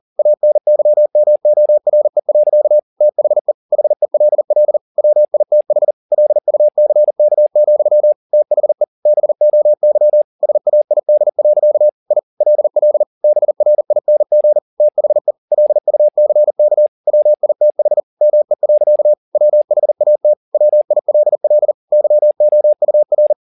Morse code, heard as a keyed tone, about 40 wpm, 600 Hz. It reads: ANYMORE. THE HELL WITH LUCK, THE BOY SAID. I LL BRING THE LUCK WITH ME. WHAT WILL YOUR
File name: the old man and the sea 2 40wpm